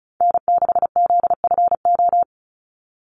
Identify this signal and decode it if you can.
Morse code, heard as a keyed tone, about 34 wpm, 700 Hz. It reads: N6ZFO
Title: Call Signs Repeated 3x - 35wpm